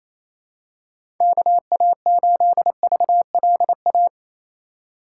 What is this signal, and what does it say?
Morse code, heard as a keyed tone, about 28 wpm, 700 Hz. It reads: KA8VLA
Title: USA Call Signs - 6 Characters - 3x Character Spacing - 28wpm